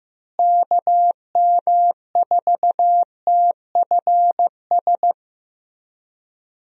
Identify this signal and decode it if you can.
Morse code, heard as a keyed tone, about 16 wpm, 700 Hz. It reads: KM4TFS
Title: USA Call Signs - 6 Characters - Repeated 3x - 15wpm